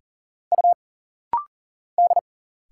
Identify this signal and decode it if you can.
Morse code, heard as a keyed tone, about 40 wpm, 700 Hz. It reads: U D
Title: Single Letters - Rapid-Fire 40wpm